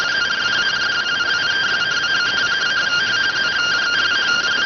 Two-tone teleprinter signal, about 170 Hz shift.